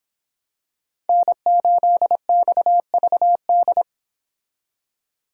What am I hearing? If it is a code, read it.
Morse code, heard as a keyed tone, about 25 wpm, 700 Hz. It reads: N8XVD